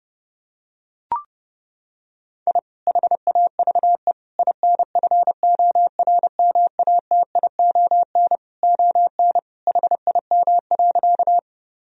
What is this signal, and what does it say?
Morse code, heard as a keyed tone, about 30 wpm, 700 Hz. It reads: I HAVE INFORMATION ON HIM.
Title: Sentences from Top 1000 Words - Review Part 3 - 30wpm